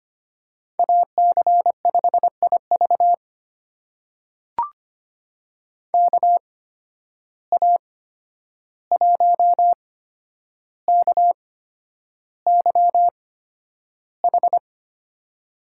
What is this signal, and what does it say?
Morse code, heard as a keyed tone, about 25 wpm, spaced as for 10 wpm, 700 Hz. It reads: AC5IV KA1KYH
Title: USA Call Signs - 8x Character Spacing - 25wpm